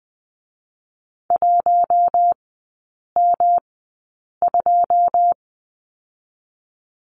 Morse code, heard as a keyed tone, about 20 wpm, 700 Hz. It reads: 1 M 2
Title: 107 Lesson 18e - Letter m - Three Characters 20wpm